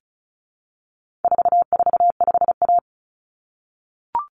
Morse code, heard as a keyed tone, about 34 wpm, 700 Hz. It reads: 445A
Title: Common Cut Numbers - 4-Digit 35wpm